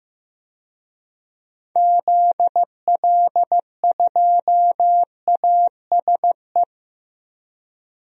Morse code, heard as a keyed tone, about 15 wpm, 700 Hz. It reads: ZL2ASE
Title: DXCC-100 Call Signs - 6 Characters - 15wpm